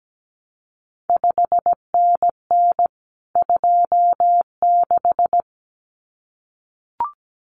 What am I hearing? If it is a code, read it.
Morse code, heard as a keyed tone, about 17 wpm, 700 Hz. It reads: 5NN 26
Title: CQ WW DX Contest Exchange Encore - 17wpm